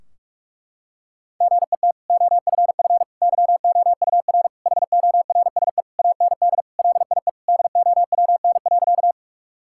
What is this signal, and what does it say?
Morse code, heard as a keyed tone, about 41 wpm, 700 Hz. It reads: GET OFF YOUR HORSE AND LIE DOWN.
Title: Sentences from Top 1600 Words 45wpm